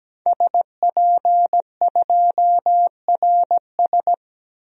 Morse code, heard as a keyed tone, about 17 wpm, 700 Hz. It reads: SP2RS
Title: DXCC-100 Call Signs - 5 Characters - 17wpm